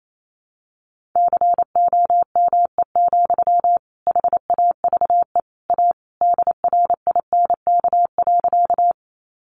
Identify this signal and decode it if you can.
Morse code, heard as a keyed tone, about 28 wpm, 700 Hz. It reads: COME, HAVE A DRINK.
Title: Sentences from Top 1500 Words 28wpm